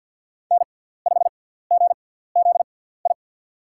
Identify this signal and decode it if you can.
Morse code, heard as a keyed tone, about 50 wpm, spaced as for 17 wpm, 700 Hz. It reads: N5GZI
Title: USA Call Signs - 5 Characters - 6x Character Spacing - 50wpm